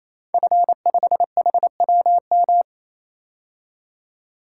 Morse code, heard as a keyed tone, about 27 wpm, 700 Hz. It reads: F5HWM
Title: Code Groups - 5 Letter:Numbers - 28wpm